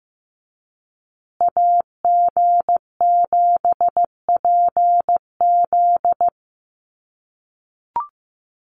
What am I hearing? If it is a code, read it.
Morse code, heard as a keyed tone, about 14 wpm, 700 Hz. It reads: AG7PZ
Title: USA Call Signs Repeated 2x 15wpm